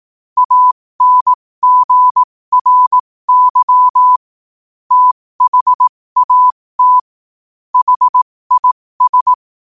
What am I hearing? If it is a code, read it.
Morse code, heard as a keyed tone, about 18 wpm, spaced as for 15 wpm, 1000 Hz. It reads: ANGRY THAT HIS
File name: qotdfiles / 15wpm